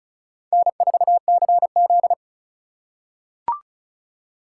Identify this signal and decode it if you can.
Morse code, heard as a keyed tone, about 36 wpm, 700 Hz. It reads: N4CZ
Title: USA Call Signs - 4 Characters - 8x Character Spacing - 35wpm